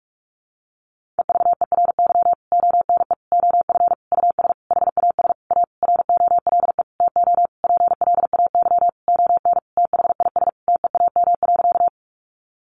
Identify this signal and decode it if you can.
Morse code, heard as a keyed tone, about 44 wpm, 700 Hz. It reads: EVERY ONE OF US HAS A ROLE TO PLAY ON THIS TEAM.
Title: Sentences from Top 1500 Words 45wpm